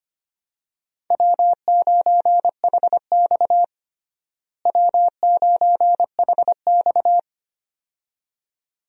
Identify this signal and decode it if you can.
Morse code, heard as a keyed tone, about 25 wpm, 700 Hz. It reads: W9HX W9HX
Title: USA Call Signs - 4 Characters - Repeated 2x - 25wpm